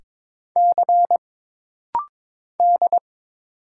Morse code, heard as a keyed tone, about 22 wpm, 700 Hz. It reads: C D
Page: This rapid-fire timing is geared to push your ICR (Instant Character Recognition) skills.